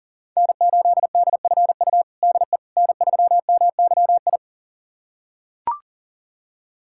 Morse code, heard as a keyed tone, about 37 wpm, 700 Hz. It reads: N8DFU DE N3MYI
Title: USA Callsign DE Callsign - 5 Characters - 40wpm